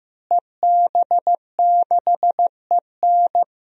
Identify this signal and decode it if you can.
Morse code, heard as a keyed tone, about 15 wpm, 700 Hz. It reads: EB6EN
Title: DXCC-100 Call Signs - 5 Characters - Repeated 2x - 15wpm